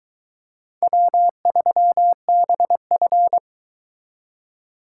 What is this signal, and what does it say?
Morse code, heard as a keyed tone, about 22 wpm, 700 Hz. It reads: W3BF